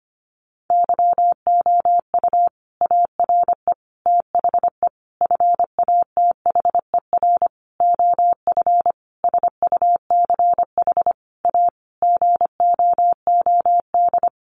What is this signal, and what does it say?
Morse code, heard as a keyed tone, about 25 wpm, 700 Hz. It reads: YOU ARE THE FATHER OF SUCH A GOOD
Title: Sentences from Top 1500 Words 25wpm